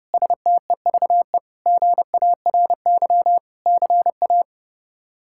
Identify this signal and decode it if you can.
Morse code, heard as a keyed tone, about 32 wpm, 700 Hz. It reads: STEVE GARY CA
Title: K1USN SST QSO Practice - 30wpm